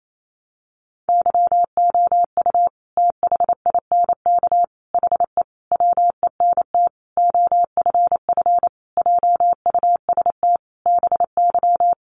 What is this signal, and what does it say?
Morse code, heard as a keyed tone, about 28 wpm, 700 Hz. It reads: YOU THINK HE WENT OFF JUST BY
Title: Sentences from Top 800 Words 28wpm